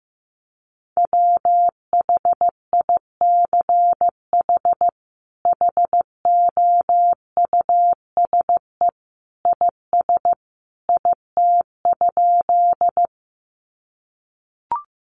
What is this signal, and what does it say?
Morse code, heard as a keyed tone, about 15 wpm, 700 Hz. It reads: WHICH HOUSE IS IT?
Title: Sentences from Top 200 Words 15wpm